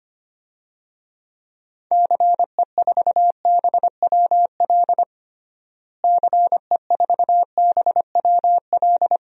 Morse code, heard as a keyed tone, about 25 wpm, 700 Hz. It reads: CE4BWL CE4BWL
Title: DXCC-100 Call Signs - 6 Characters - Repeated 2x - 25wpm